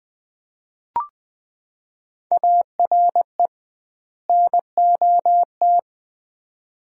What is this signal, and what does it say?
Morse code, heard as a keyed tone, about 21 wpm, 700 Hz. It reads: ARE NOT
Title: 2x Word-Spacing - Sets of 2 Words - 20wpm